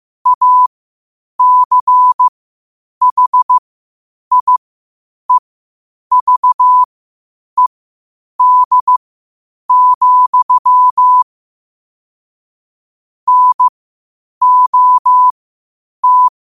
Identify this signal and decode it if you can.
Morse code, heard as a keyed tone, about 15 wpm, spaced as for 8 wpm, 1000 Hz. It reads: ACHIEVED, NOT
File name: qotdfiles / 8wpm